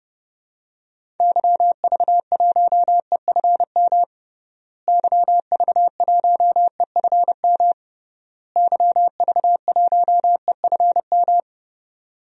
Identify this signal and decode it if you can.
Morse code, heard as a keyed tone, about 30 wpm, 700 Hz. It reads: YV1EFM YV1EFM YV1EFM
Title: DXCC-100 Call Signs - 6 Characters - Repeated 3x - 30wpm